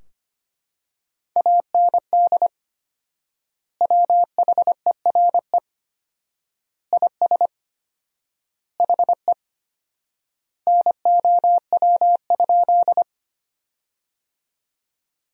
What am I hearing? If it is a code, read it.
Morse code, heard as a keyed tone, about 25 wpm, 700 Hz. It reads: AND WHERE IS HE NOW?
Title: 4x Word-Spacing - Sentences from Top 200 Words - 25wpm